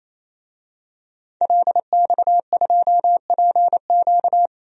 Morse code, heard as a keyed tone, about 28 wpm, 700 Hz. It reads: LX2PQ
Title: DXCC-100 Call Signs - 5 Characters - 28wpm